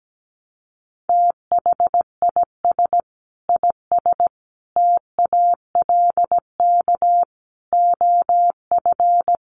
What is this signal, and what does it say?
Morse code, heard as a keyed tone, about 17 wpm, 700 Hz. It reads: THIS IS TALK OF
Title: Sets of 5 Words 17wpm